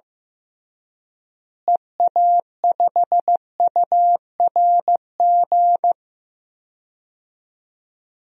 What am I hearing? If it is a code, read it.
Morse code, heard as a keyed tone, about 15 wpm, 700 Hz. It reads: EA5URG
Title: DXCC-100 Call Signs - 6 Characters - Repeated 3x - 15wpm